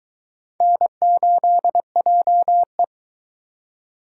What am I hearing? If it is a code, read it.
Morse code, heard as a keyed tone, about 21 wpm, 700 Hz. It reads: N8JE